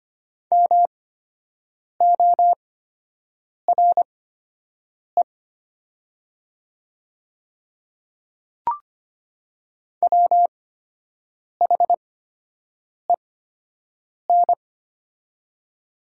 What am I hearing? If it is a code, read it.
Morse code, heard as a keyed tone, about 25 wpm, spaced as for 7 wpm, 700 Hz. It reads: MORE WHEN
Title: 8x Character Spacing - Top 100 Words - 25wpm